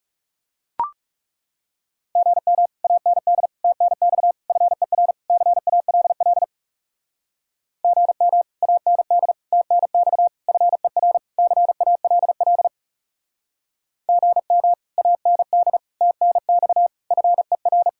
Morse code, heard as a keyed tone, about 37 wpm, 700 Hz. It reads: GM AND TNX FER CALL GM AND TNX FER CALL GM AND TNX FER
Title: Speed-Racing Common QSO Phrases - 30wpm